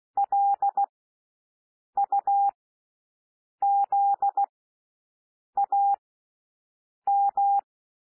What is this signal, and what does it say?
Morse code, heard as a keyed tone, about 16 wpm, spaced as for 6 wpm, 800 Hz. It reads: LUZAM
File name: cwmp3 / L1 SOUND LETTERS 16 wpm